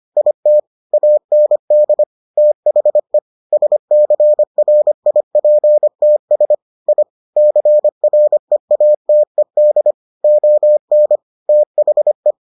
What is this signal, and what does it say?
Morse code, heard as a keyed tone, about 25 wpm, 600 Hz. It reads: IT AND THE SCRIPTS I CREATED ON THE